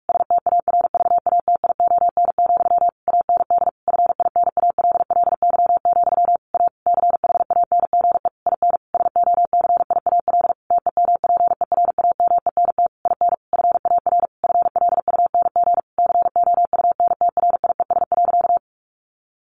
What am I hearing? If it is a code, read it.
Morse code, heard as a keyed tone, about 44 wpm, 700 Hz. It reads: STARVATION, AND FINALLY, A CHANGE IN SOCIAL TEMPERAMENT IN FAR FLUNG COUNTRIES.
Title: A Sound Of Thunder 45wpm